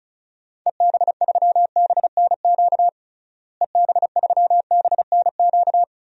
Morse code, heard as a keyed tone, about 34 wpm, 700 Hz. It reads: EB3BNQ EB3BNQ
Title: DXCC-100 Call Signs - 6 Characters - Repeated 2x - 35wpm